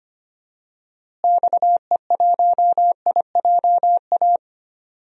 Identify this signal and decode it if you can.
Morse code, heard as a keyed tone, about 25 wpm, 700 Hz. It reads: XE1IJA
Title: DXCC-100 Call Signs - 6 Characters - Repeated 2x - 25wpm